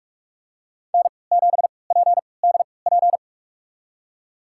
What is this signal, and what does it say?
Morse code, heard as a keyed tone, about 42 wpm, spaced as for 25 wpm, 700 Hz. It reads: N7PDP
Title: USA Call Signs - 3x Character Spacing - 45wpm